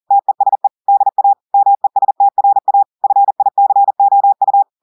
Morse code, heard as a keyed tone, about 40 wpm, 800 Hz. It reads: TESE DA MESTRA FICOU
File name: audios / portugais / portugais texte en clair 1 vitesse 40 mots minutes